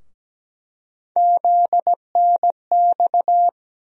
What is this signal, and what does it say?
Morse code, heard as a keyed tone, about 17 wpm, 700 Hz. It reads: ZNX
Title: USA Call Sign Postfixes - 3 Letters 17wpm